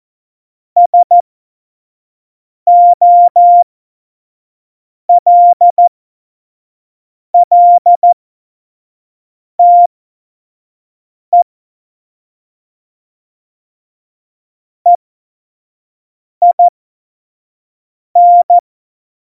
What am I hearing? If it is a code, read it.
Morse code, heard as a keyed tone, about 14 wpm, spaced as for 5 wpm, 700 Hz. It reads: SOLLTE EIN
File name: Morsekurs / Witze / Witz 01. Oktober 2019, 25 F70